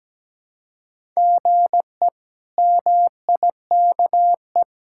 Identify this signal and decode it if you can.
Morse code, heard as a keyed tone, about 17 wpm, 700 Hz. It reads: GE MIKE
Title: K1USN SST QSO Practice - 17wpm